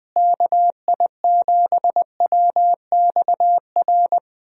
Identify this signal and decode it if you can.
Morse code, heard as a keyed tone, about 20 wpm, 700 Hz. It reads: KI7WXR
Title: USA Call Signs - 6 Characters - Repeated 3x - 20wpm